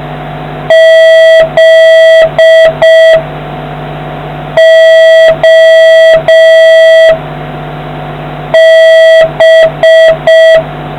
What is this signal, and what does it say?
Morse code, heard as a keyed tone, about 5 wpm, 1950 Hz, with noise. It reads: Z O B